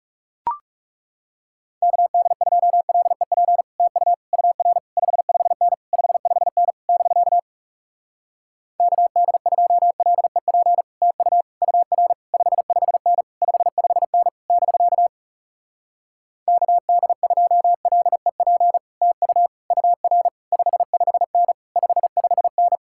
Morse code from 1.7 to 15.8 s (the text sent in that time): KD2LEP TU UR 55N 55N <BK> KD2LEP TU UR 55N 55N <BK>